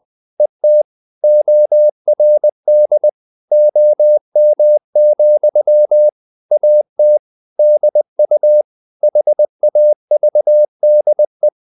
Morse code, heard as a keyed tone, about 20 wpm, 600 Hz. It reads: ET ORD OM, AT DU HAVDE
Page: Ferie 20 wpm